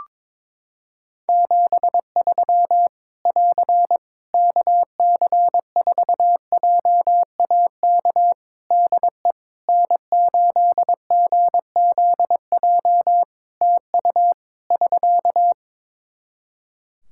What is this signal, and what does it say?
Morse code, heard as a keyed tone, about 22 wpm, 700 Hz. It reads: 73 <AR> KC4JAK DE N8GZJ TU <SK>
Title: Practice QSO - Exchange 4 - Ending - 22wpm